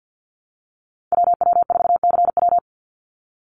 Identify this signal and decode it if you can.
Morse code, heard as a keyed tone, about 50 wpm, 700 Hz. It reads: WW4CW
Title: USA Call Signs 50wpm